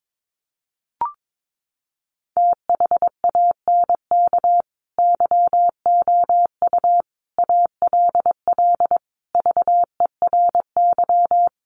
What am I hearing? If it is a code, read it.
Morse code, heard as a keyed tone, about 22 wpm, 700 Hz. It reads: THANK YOU ALL VERY
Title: Sets of 5 Words Encore 22wpm